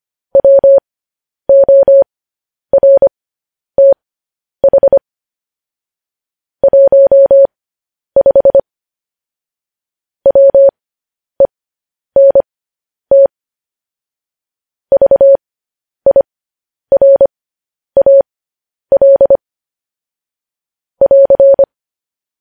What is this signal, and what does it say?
Morse code, heard as a keyed tone, about 25 wpm, spaced as for 10 wpm, 550 Hz. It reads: WORTH 15 WENT VIRAL <AR>
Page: news headlines in morse code at 10 wpm